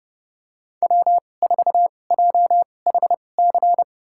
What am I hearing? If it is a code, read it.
Morse code, heard as a keyed tone, about 30 wpm, spaced as for 22 wpm, 700 Hz. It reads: W4JHC
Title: USA Call Signs - 5 Characters - 2x Character Spacing - 30wpm